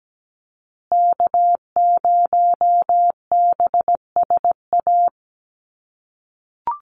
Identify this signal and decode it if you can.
Morse code, heard as a keyed tone, about 17 wpm, 700 Hz. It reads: K0BSA
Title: USA Call Signs Repeated 3x 17wpm